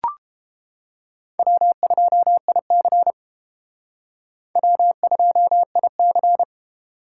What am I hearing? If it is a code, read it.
Morse code, heard as a keyed tone, about 30 wpm, 700 Hz. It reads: W2IC W2IC